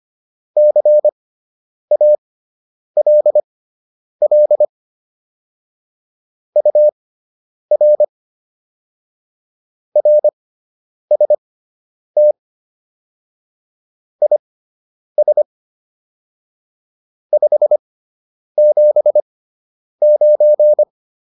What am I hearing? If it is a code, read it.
Morse code, heard as a keyed tone, about 25 wpm, spaced as for 9 wpm, 600 Hz. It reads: CALL UR RST IS 579